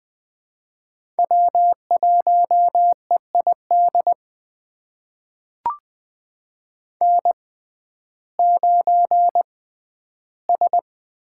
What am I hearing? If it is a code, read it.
Morse code, heard as a keyed tone, about 20 wpm, spaced as for 12 wpm, 700 Hz. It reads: W1EID N9S
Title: USA Call Signs - 5 Characters - 6x Character Spacing - 20wpm